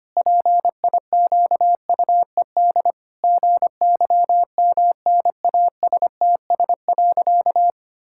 Morse code, heard as a keyed tone, about 25 wpm, 700 Hz. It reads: PIQUED GYMNASTS.
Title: Pangram Challenge v2 - 25wpm